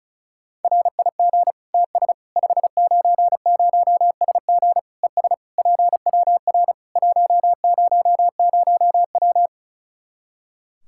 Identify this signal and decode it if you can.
Morse code, heard as a keyed tone, about 36 wpm, 700 Hz. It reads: RIG TS 590SG ES PWR 100W
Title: Practice QSO - Exchange 2 - Info - 35wpm